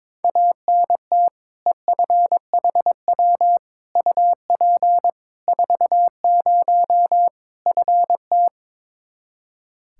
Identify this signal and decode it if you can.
Morse code, heard as a keyed tone, about 22 wpm, 700 Hz. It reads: ANT EFHW UP 40 FT
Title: K1USN SST QSO Practice 22wpm